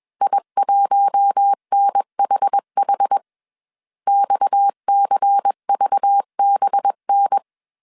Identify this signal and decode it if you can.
Morse code, heard as a keyed tone, about 21 wpm, 800 Hz. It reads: I1NHH XCVBN
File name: CW / DET / 1 6 1500C S A 20wpm 2048